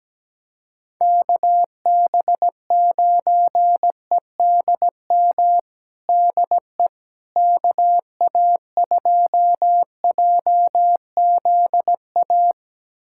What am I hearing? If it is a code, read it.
Morse code, heard as a keyed tone, about 17 wpm, 700 Hz. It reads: KB9EDM DE KA2JZA
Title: USA Callsign DE Callsign 17wpm